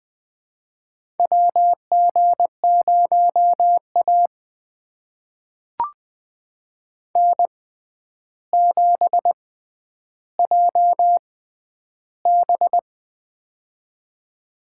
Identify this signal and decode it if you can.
Morse code, heard as a keyed tone, about 20 wpm, spaced as for 10 wpm, 700 Hz. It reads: WG0A N7JB
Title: USA Call Signs - 4 Characters - 6x Character Spacing - 20wpm